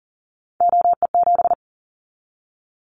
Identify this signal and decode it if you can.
Morse code, heard as a keyed tone, about 41 wpm, 700 Hz. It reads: OE7
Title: Callsign Prefixes for Top 50 DXCC - 40wpm